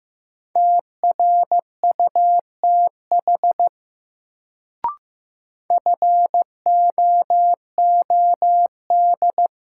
Morse code, heard as a keyed tone, about 15 wpm, 700 Hz. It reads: TRUTH FOOD
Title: Sets of 5 Words - New Word Drill - 15wpm